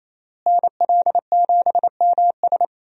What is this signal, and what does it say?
Morse code, heard as a keyed tone, about 28 wpm, 700 Hz. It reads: NL7MS
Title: USA Call Signs - 5 Characters - Repeated 3x - 28wpm